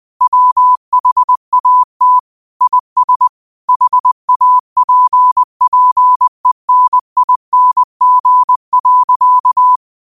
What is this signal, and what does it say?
Morse code, heard as a keyed tone, about 20 wpm, 1000 Hz. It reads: WHAT IS HAPPENING.
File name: qotdfiles / 20wpm